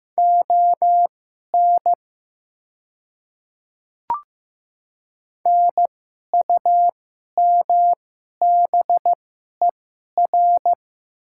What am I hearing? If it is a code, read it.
Morse code, heard as a keyed tone, about 15 wpm, spaced as for 11 wpm, 700 Hz. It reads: ON NUMBER
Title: 2x Character Spacing - Top 100 Words - 15wpm